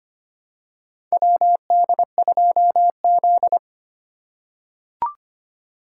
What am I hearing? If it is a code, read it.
Morse code, heard as a keyed tone, about 25 wpm, 700 Hz. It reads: WD2Z
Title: USA Call Signs - 4 Characters - 3x Character Spacing - 25wpm